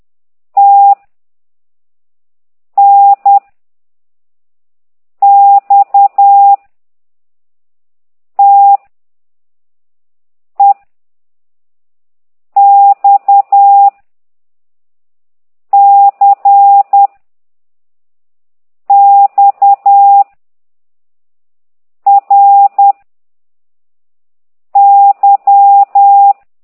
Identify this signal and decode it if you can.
Morse code, heard as a keyed tone, about 10 wpm, 800 Hz. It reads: T N X T E X C X R Y